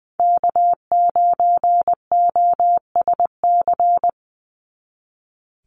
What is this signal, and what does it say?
Morse code, heard as a keyed tone, about 20 wpm, 700 Hz. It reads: K9OSC
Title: NA SOTA Chaser Callsigns 20wpm